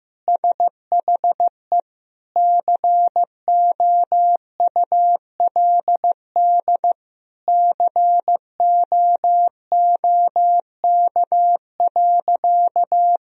Morse code, heard as a keyed tone, about 15 wpm, 700 Hz. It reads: SHE COULD COOK.
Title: Sentences from Top 1000 Words - Review Part 4 - 15wpm